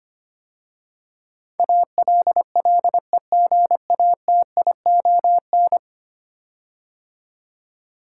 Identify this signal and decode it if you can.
Morse code, heard as a keyed tone, about 25 wpm, 700 Hz. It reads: ALLEGATION
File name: 10 Letter Words - First 3 Letters Given - 25wpm